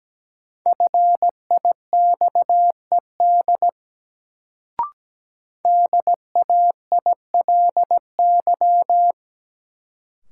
Morse code, heard as a keyed tone, about 17 wpm, 700 Hz. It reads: FIXED DAILY
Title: Sentences from Top 1100 Words - New Word Drill - 17wpm